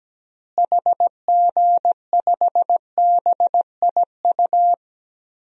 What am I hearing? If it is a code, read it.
Morse code, heard as a keyed tone, about 17 wpm, 700 Hz. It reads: HG5BIU
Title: DXCC-100 Call Signs - 6 Characters - Repeated 3x - 17wpm